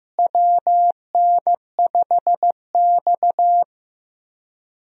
Morse code, heard as a keyed tone, about 15 wpm, 700 Hz. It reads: WN5X
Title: USA Call Signs - 4 Characters - Repeated 3x - 15wpm